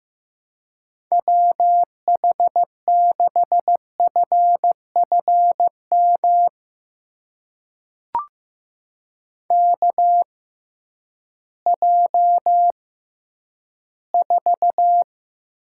Morse code, heard as a keyed tone, about 15 wpm, spaced as for 10 wpm, 700 Hz. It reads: WH6FFM KJ4
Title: USA Call Signs - 6x Character Spacing - 15wpm